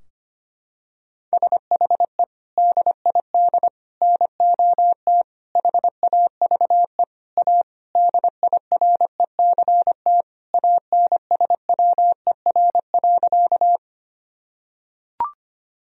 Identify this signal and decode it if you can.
Morse code, heard as a keyed tone, about 25 wpm, 700 Hz. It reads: SHE DID NOT HAVE A DIRECT ANSWER.
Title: Sentences from Top 1000 Words - Review Part 2 - 25wpm